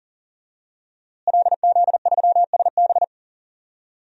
Morse code, heard as a keyed tone, about 40 wpm, 700 Hz. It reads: LZ3SB
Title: DXCC-100 Call Signs - 5 Characters - Repeated 3x - 40wpm